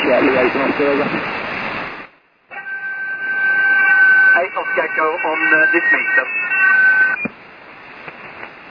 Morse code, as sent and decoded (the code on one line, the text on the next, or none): none